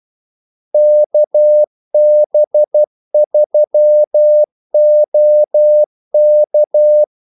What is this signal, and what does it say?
Morse code, heard as a keyed tone, about 12 wpm, 600 Hz. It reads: KB3OK
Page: 2 ekstra ordmellemrum mellem kaldesignalerne